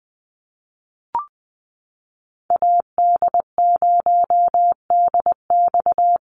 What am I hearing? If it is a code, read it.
Morse code, heard as a keyed tone, about 20 wpm, 700 Hz. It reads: AD0DX
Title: Call Signs - 20wpm